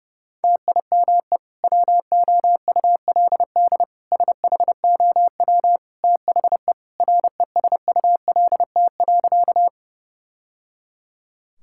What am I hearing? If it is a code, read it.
Morse code, heard as a keyed tone, about 30 wpm, 700 Hz. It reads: TIME WOULD SHOW THE RESULT.
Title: Sentences from Top 1000 Words - Review Part 5 - 30wpm